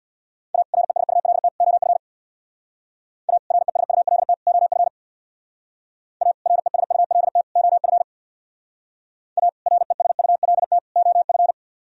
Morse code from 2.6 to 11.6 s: A RESULT OF A RESULT OF A RESULT OF

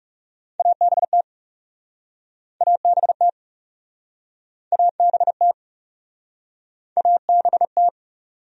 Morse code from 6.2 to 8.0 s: ABT

